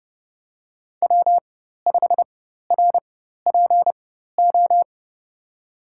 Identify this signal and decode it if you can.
Morse code, heard as a keyed tone, about 30 wpm, spaced as for 14 wpm, 700 Hz. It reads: W5RPO
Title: USA Call Signs - 5 Characters - 4x Character Spacing - 30wpm